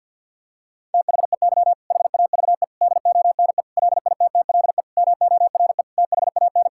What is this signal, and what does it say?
Morse code, heard as a keyed tone, about 52 wpm, 700 Hz. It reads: THEY HAVE DONE LITTLE MORE THAN
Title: Sentences from Top 1000 Words - Review Part 5 - 50wpm